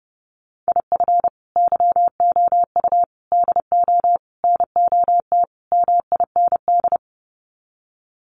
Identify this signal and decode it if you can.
Morse code, heard as a keyed tone, about 30 wpm, 700 Hz. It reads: IF YOU DO NOT MIND
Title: Sets of 5 Words 30wpm